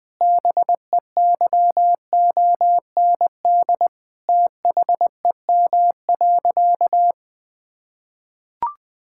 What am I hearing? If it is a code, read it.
Morse code, heard as a keyed tone, about 21 wpm, 700 Hz. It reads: BEYOND THEM.
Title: Sentences from Top 1000 Words - Review Part 6 - 20wpm